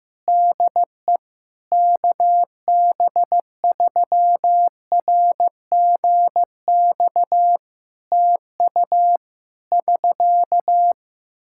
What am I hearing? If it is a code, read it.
Morse code, heard as a keyed tone, about 15 wpm, 700 Hz. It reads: DE KB3RGX TU <SK>
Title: Practice QSO - Exchange 4 - Ending - 15wpm